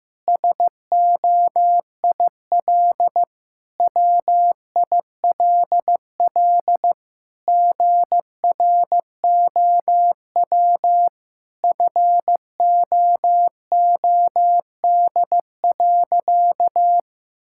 Morse code, heard as a keyed tone, about 16 wpm, 700 Hz. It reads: SOIL WILL GROW FOOD.
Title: Sentences from Top 1000 Words - Review Part 6 - 15wpm